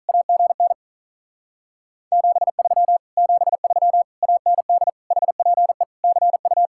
Morse code, heard as a keyed tone, about 46 wpm, 700 Hz. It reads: AGN 73 73 AND HPE CU